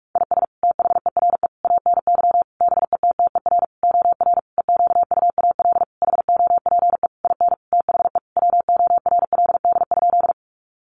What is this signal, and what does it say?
Morse code, heard as a keyed tone, about 44 wpm, 700 Hz. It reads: IS THERE ANY BETTER OR EQUAL HOPE IN THE WORLD?
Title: Sentences from Top 1400 Words 45wpm